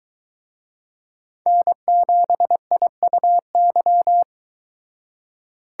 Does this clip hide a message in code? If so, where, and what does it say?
Yes: Morse code from 1.5 to 4.2 s, heard as a keyed tone, about 23 wpm, 700 Hz. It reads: N7IUY